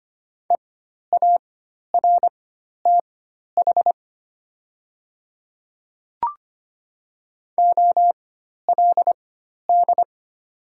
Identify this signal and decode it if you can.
Morse code, heard as a keyed tone, about 25 wpm, spaced as for 12 wpm, 700 Hz. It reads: EARTH OLD
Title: 4x Character Spacing - Top 200 Words - 25wpm